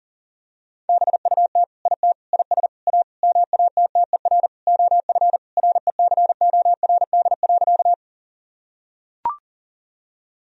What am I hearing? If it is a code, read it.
Morse code, heard as a keyed tone, about 37 wpm, 700 Hz. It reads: BUT IT IS A MATTER OF RECORD.
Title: Sentences from Top 1000 Words - Review Part 3 - 40wpm